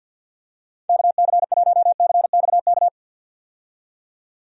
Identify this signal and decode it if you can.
Morse code, heard as a keyed tone, about 51 wpm, 700 Hz. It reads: KC1CXK
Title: USA Call Signs - 6 Characters - 50wpm